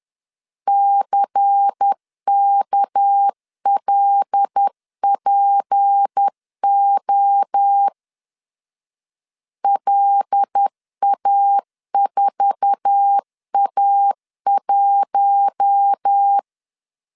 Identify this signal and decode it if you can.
Morse code, heard as a keyed tone, about 10 wpm, 800 Hz. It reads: CKLPO LA4A1
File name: CW / DET / 1 9 1500C S A 10wpm 4511